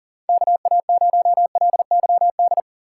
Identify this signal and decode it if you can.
Morse code, heard as a keyed tone, about 40 wpm, 700 Hz. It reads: KA0LYD
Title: USA Call Signs - 2x Character Spacing - 40wpm